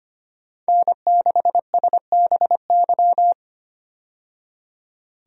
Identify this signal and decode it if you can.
Morse code, heard as a keyed tone, about 25 wpm, 700 Hz. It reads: N6SBY
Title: USA Call Signs - 5 Characters - 25wpm